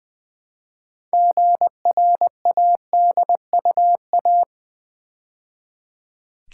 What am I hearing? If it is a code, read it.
Morse code, heard as a keyed tone, about 20 wpm, 700 Hz. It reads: GRADUA
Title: Build A Word - 2 Letters At A Time - 8-Letter Words - 20wpm